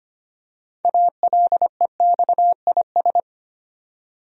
Morse code, heard as a keyed tone, about 25 wpm, 700 Hz. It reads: ALEXIS
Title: USA Common Names - 25wpm